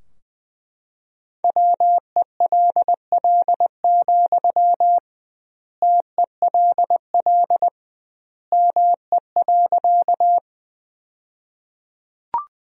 Morse code, heard as a keyed tone, about 20 wpm, 700 Hz. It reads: WELL, TELL ME.
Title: 2x Word-Spacing - Sentences from Top 200 Words - 20wpm